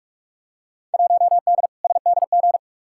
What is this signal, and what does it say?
Morse code, heard as a keyed tone, about 45 wpm, 700 Hz. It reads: 1D SDG
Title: Field Day Exchanges - 45wpm